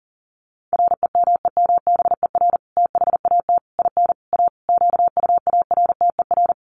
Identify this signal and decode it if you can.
Morse code, heard as a keyed tone, about 40 wpm, 700 Hz. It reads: REMEMBER THAT IN A QUARTER
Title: Sentences from Top 1300 Words 40wpm